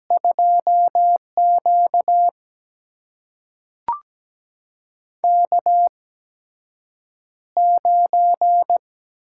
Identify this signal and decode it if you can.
Morse code, heard as a keyed tone, about 17 wpm, spaced as for 7 wpm, 700 Hz. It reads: 2Q K9
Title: USA Call Signs - 4 Characters - 8x Character Spacing - 17wpm